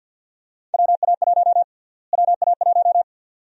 Morse code, heard as a keyed tone, about 50 wpm, 700 Hz. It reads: WA1 WA1
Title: USA Call Sign Prefixes - Repeated 2x - 50wpm